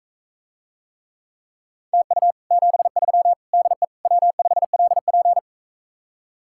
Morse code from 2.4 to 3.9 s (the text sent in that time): 73 DE